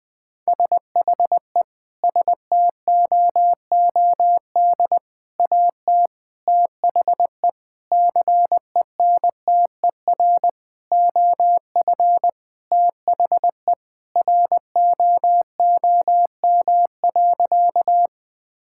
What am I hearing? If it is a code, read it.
Morse code, heard as a keyed tone, about 19 wpm, 700 Hz. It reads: SHE STOOD AT THE CENTER OF THE ROOM.
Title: Sentences with 5-Word Phrases - 20wpm